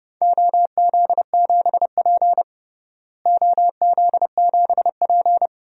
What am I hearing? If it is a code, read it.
Morse code, heard as a keyed tone, about 32 wpm, 700 Hz. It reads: OZ7P OZ7P
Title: DXCC-100 Call Signs - 4 Characters - Repeated 2x - 30wpm